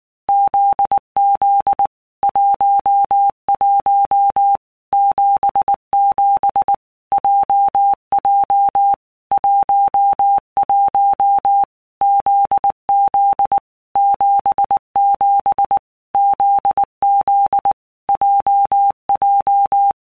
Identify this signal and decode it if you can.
Morse code, heard as a keyed tone, about 19 wpm, 800 Hz. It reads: ZZ 11 77 JJ 11 ZZ 77 ZZ JJ
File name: All Groups 1-minute Blocks / Groups@18 / JZ17